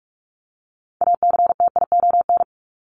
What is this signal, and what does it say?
Morse code, heard as a keyed tone, about 44 wpm, 700 Hz. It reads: ACTION
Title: Top 501-750 Words 45wpm